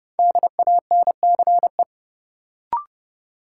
Morse code, heard as a keyed tone, about 30 wpm, 700 Hz. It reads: DANCE
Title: Sentences from Top 700 Words - New Word Drill - 30wpm